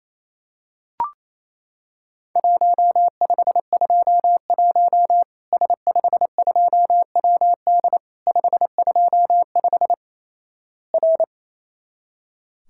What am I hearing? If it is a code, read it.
Morse code, heard as a keyed tone, about 28 wpm, 700 Hz. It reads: 1521 S52WD 525 R
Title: Worked All Europe DX Contest - QTC Exchange - 28wpm